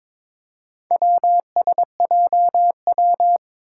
Morse code, heard as a keyed tone, about 22 wpm, 700 Hz. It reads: WSJW
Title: Four Challenging Letter-Number Combinations 22wpm